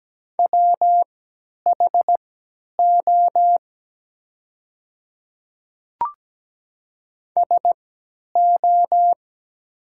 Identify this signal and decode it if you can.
Morse code, heard as a keyed tone, about 17 wpm, spaced as for 10 wpm, 700 Hz. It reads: WHO SO
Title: 3x Character Spacing - Top 100 Words - 17wpm